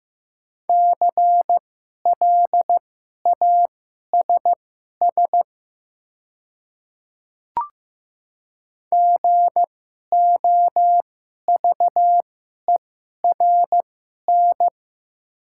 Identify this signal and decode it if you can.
Morse code, heard as a keyed tone, about 15 wpm, spaced as for 11 wpm, 700 Hz. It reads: CLASS GOVERN
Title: 2x Character Spacing - Top 500 Words - 15wpm